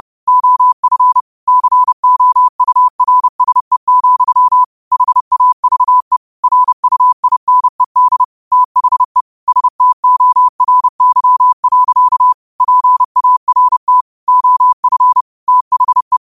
Morse code, heard as a keyed tone, about 31 wpm, 1000 Hz. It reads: OF COURSE, HAVE RUINED THE STORY. PART OF THE
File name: qotdfiles / 30wpm